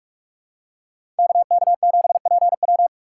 Morse code, heard as a keyed tone, about 46 wpm, 700 Hz. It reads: KK7PW
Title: USA Call Signs - 5 Characters - 4x Character Spacing - 45wpm